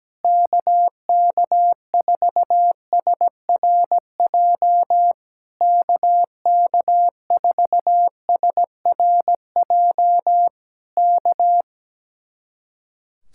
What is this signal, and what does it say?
Morse code, heard as a keyed tone, about 17 wpm, 700 Hz. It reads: KK4SRJ KK4SRJ K
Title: Speed-Racing Common QSO Phrases - 17wpm